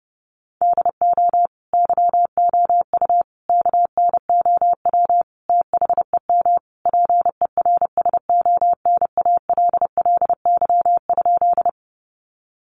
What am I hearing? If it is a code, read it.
Morse code, heard as a keyed tone, about 30 wpm, 700 Hz. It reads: DO YOU KNOW THEM PERSONALLY?
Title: Sentences from Top 1400 Words 30wpm